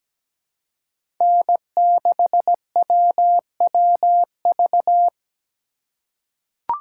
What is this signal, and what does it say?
Morse code, heard as a keyed tone, about 17 wpm, 700 Hz. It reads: N6WWV
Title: USA Call Signs - 5 Characters - 3x Character Spacing - 17wpm